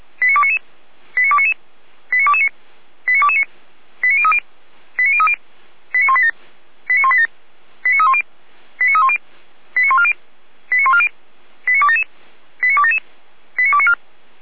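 Five-tone selective call (ZVEI), heard as quick runs of five tones, tones of about 70 ms. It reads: 89299 89299 89229 89229 89030 89030 89187 89187 89310 89310 89140 89140 89277 89277 89294
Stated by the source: ZVEI five-tone call received by SDR.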